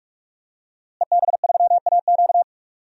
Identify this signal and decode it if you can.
Morse code, heard as a keyed tone, about 47 wpm, 700 Hz. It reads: EB3AQ
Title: DXCC-100 Call Signs - 5 Characters - Repeated 2x - 45wpm